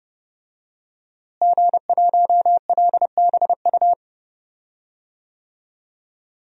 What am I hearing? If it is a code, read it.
Morse code, heard as a keyed tone, about 32 wpm, 700 Hz. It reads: G1LBU
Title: DXCC-100 Call Signs - 5 Characters - Repeated 2x - 30wpm